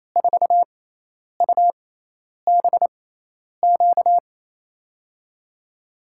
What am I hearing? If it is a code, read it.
Morse code, heard as a keyed tone, about 28 wpm, spaced as for 10 wpm, 700 Hz. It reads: 4UBQ
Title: USA Call Signs - 6x Character Spacing - 28wpm